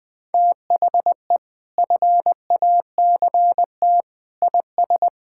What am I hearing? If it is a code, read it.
Morse code, heard as a keyed tone, about 21 wpm, 700 Hz. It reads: THE FACT IS
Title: Sentences with 2-Word Phrases - Spoken Beginning - 20wpm